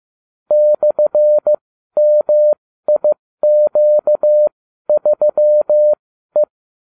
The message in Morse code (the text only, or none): /MIQ3E